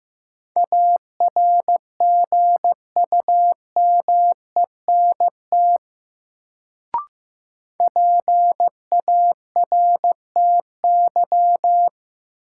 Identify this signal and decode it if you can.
Morse code, heard as a keyed tone, about 15 wpm, 700 Hz. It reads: ARGUMENT PARTY
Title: Sentences from Top 1400 Words - New Word Drill - 15wpm